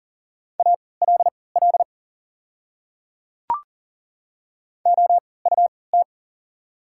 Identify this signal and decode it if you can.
Morse code, heard as a keyed tone, about 40 wpm, spaced as for 23 wpm, 700 Hz. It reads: ALL OUT
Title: 3x Character Spacing - Top 100 Words - 40wpm